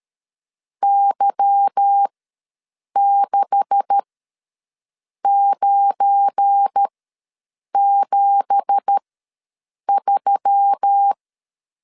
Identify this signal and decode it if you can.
Morse code, heard as a keyed tone, about 13 wpm, spaced as for 7 wpm, 800 Hz. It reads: Y6973
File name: LEZIONI / 9 2867 ws 12wpm 3431